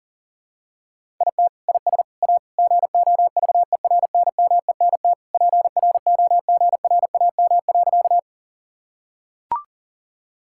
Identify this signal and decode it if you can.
Morse code, heard as a keyed tone, about 40 wpm, 700 Hz. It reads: IT IS A GOVERNMENT PROGRAM.
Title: Sentences from Top 1000 Words - Review Part 5 - 40wpm